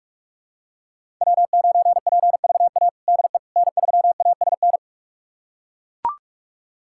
Morse code, heard as a keyed tone, about 41 wpm, 700 Hz. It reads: W9PVA DE N3ASN
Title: USA Callsign DE Callsign - 5 Characters - 45wpm